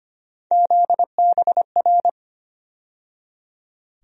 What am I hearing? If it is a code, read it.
Morse code, heard as a keyed tone, about 25 wpm, 700 Hz. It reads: ZBR
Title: USA Call Sign Postfixes - 3 Letters 25wpm